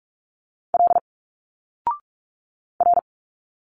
Morse code, heard as a keyed tone, about 46 wpm, 700 Hz. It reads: L R
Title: Single Letters 45wpm